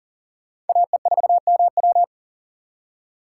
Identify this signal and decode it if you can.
Morse code, heard as a keyed tone, about 51 wpm, 700 Hz. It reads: AE4MW